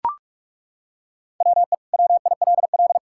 Morse code, heard as a keyed tone, about 42 wpm, 700 Hz. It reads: WE WILL
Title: Sentences from Top 1000 Words - Review Part 6 - 45wpm